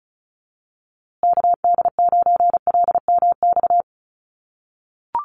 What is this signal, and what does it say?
Morse code, heard as a keyed tone, about 34 wpm, 700 Hz. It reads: KD9LMX
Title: USA Call Signs Repeated 3x 35wpm